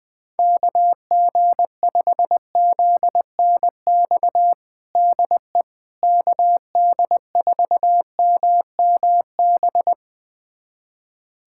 Morse code, heard as a keyed tone, about 20 wpm, 700 Hz. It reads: KG5ZNX DE KD4MMB
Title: USA Callsign DE Callsign 20wpm